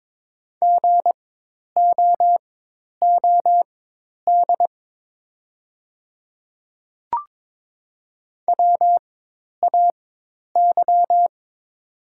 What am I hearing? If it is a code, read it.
Morse code, heard as a keyed tone, about 22 wpm, spaced as for 10 wpm, 700 Hz. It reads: GOOD WAY
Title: 4x Character Spacing - Top 200 Words - 22wpm